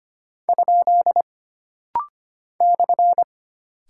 Morse code, heard as a keyed tone, about 25 wpm, 700 Hz. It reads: ? /
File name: Single Punctuation-Prosign - Rapid-Fire - 25wpm